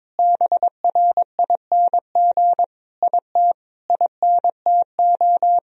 Morse code, heard as a keyed tone, about 22 wpm, 700 Hz. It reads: BRING IT INTO
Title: Sentences from Top 1000 Words - Review Part 4 - 22wpm